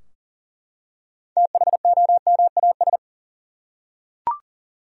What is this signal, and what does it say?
Morse code, heard as a keyed tone, about 40 wpm, 700 Hz. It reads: THOMAS
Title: USA Common Names - 40wpm